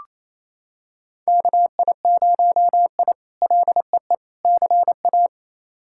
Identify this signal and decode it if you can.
Morse code, heard as a keyed tone, about 27 wpm, 700 Hz. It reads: KI0I LEE CA
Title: K1USN SST QSO Practice - 28wpm